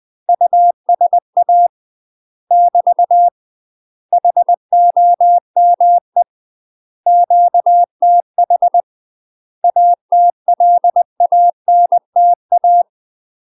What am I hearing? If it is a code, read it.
Morse code, heard as a keyed tone, about 20 wpm, 700 Hz. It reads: USA <BT> HOME QTH ATLANTA
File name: UK MASTER QSOs TRAINING / 20WPM / UK MASTER QSOs 0062